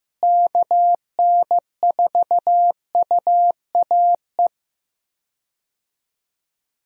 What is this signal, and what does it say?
Morse code, heard as a keyed tone, about 15 wpm, 700 Hz. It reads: KN4UAE
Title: USA Call Signs - 6 Characters - Repeated 3x - 15wpm